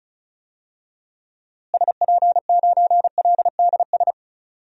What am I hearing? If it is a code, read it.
Morse code, heard as a keyed tone, about 36 wpm, 700 Hz. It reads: SP9LDS
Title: DXCC-100 Call Signs - 6 Characters - Repeated 2x - 35wpm